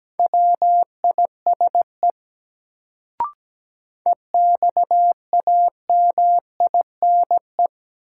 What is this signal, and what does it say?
Morse code, heard as a keyed tone, about 17 wpm, 700 Hz. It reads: WISE EXAMINE
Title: Sentences from Top 1600 Words - New Word Drill Alt - 17wpm